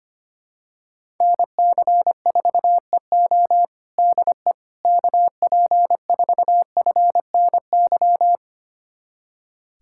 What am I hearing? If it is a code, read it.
Morse code, heard as a keyed tone, about 25 wpm, 700 Hz. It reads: NC4EO DE KP4FNY
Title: USA Callsign DE Callsign 25wpm